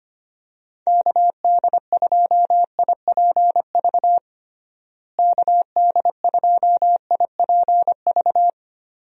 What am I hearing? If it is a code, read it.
Morse code, heard as a keyed tone, about 25 wpm, 700 Hz. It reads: KD2IPV KD2IPV
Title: USA Call Signs - 6 Characters - Repeated 3x - 25wpm